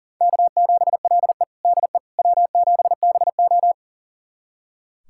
Morse code, heard as a keyed tone, about 40 wpm, 700 Hz. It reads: K7LE DE W7BO
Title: USA Callsign DE Callsign - 4 Characters - 40wpm